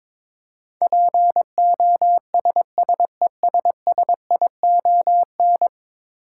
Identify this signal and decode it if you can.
Morse code, heard as a keyed tone, about 22 wpm, 700 Hz. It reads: POSSESSION
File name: practice / 22wpm / 10 Letter Words - First 3 Letters Given - 22wpm